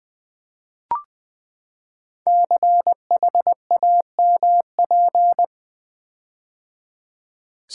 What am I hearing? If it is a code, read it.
Morse code, heard as a keyed tone, about 20 wpm, 700 Hz. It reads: CHAMP
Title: Build A Word - 5 Letters At A Time - 12-Letter Words - 20wpm